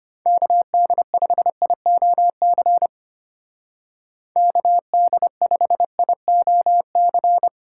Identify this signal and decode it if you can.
Morse code, heard as a keyed tone, about 30 wpm, 700 Hz. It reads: KD5IOC KD5IOC
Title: Speed Racing - USA 2x3 Call Signs 25wpm